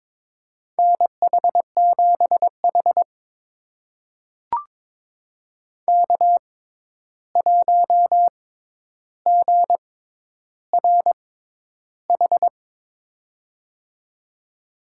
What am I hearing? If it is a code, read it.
Morse code, heard as a keyed tone, about 22 wpm, spaced as for 11 wpm, 700 Hz. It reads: NH7H K1GRH
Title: USA Call Signs - 6x Character Spacing - 22wpm